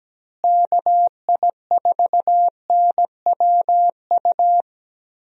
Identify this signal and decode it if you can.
Morse code, heard as a keyed tone, about 17 wpm, 700 Hz. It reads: KI4NWU
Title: USA Call Signs - 6 Characters - Repeated 3x - 17wpm